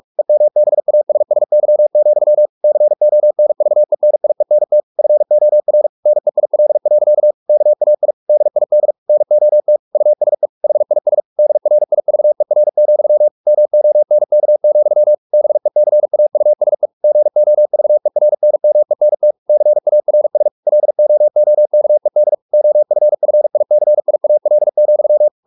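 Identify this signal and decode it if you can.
Morse code, heard as a keyed tone, about 44 wpm, 600 Hz. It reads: EMBASSY, CONVENIENT FOR NEIL. KAI DID NOT USE HIS DRIVER, MONK, BECAUSE GOVERNMENT CARS LOOKED OFFICIAL,
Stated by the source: Morse practice at 45 wpm, the novel Never by Ken Follett in English, from CW med Gnister.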